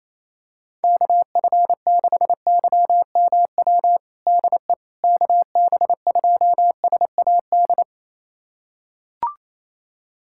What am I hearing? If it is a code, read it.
Morse code, heard as a keyed tone, about 28 wpm, 700 Hz. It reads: KF6YMW DE KB2SAD
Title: USA Callsign DE Callsign 28wpm